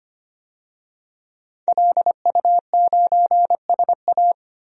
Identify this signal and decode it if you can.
Morse code, heard as a keyed tone, about 25 wpm, 700 Hz. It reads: LU9SA
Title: DXCC-100 Call Signs - 5 Characters - 25wpm